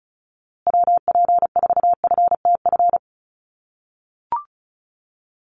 Morse code, heard as a keyed tone, about 34 wpm, 700 Hz. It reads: WP4FTF
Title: USA Call Signs Repeated 3x 35wpm